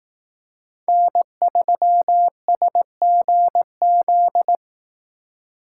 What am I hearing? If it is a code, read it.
Morse code, heard as a keyed tone, about 18 wpm, 700 Hz. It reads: N3SGZ
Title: Speed Racing - USA 1x3 Call Signs 15wpm